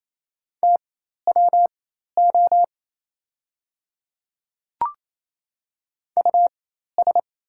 Morse code, heard as a keyed tone, about 27 wpm, spaced as for 13 wpm, 690 Hz. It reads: TWO US
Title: 4x Character Spacing - Top 200 Words - 28wpm